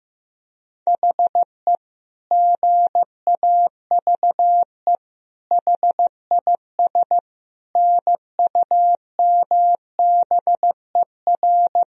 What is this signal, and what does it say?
Morse code, heard as a keyed tone, about 15 wpm, 700 Hz. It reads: HE GAVE HIS NUMBER
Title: Sentences from Top 1000 Words - Review Part 1 - 15wpm